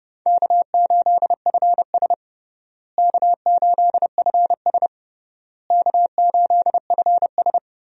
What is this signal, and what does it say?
Morse code, heard as a keyed tone, about 30 wpm, 700 Hz. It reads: K8FS K8FS K8FS
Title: USA Call Signs - 4 Characters - Repeated 3x - 30wpm